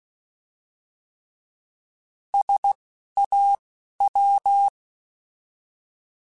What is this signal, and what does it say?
Morse code, heard as a keyed tone, about 16 wpm, spaced as for 11 wpm, 800 Hz. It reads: SAW
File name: Alphabetical / Alphabetical mp3s / Ss @10